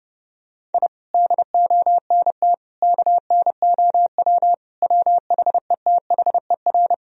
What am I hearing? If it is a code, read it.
Morse code, heard as a keyed tone, about 30 wpm, 700 Hz. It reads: I DONT KNOW WHETHER
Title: Sentences from Top 1000 Words - Review Part 3 - 30wpm